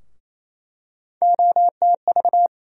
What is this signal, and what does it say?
Morse code, heard as a keyed tone, about 28 wpm, 700 Hz. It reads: OTV
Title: USA Call Sign Postfixes - 3 Letters 28wpm